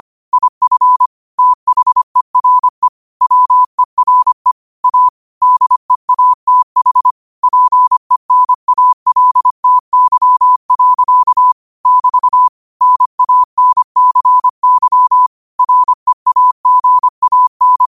Morse code, heard as a keyed tone, about 25 wpm, 1000 Hz. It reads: IF THERE WERE A DEATH PENALTY. <BT> NANCY REAGAN
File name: qotdfiles / 25wpm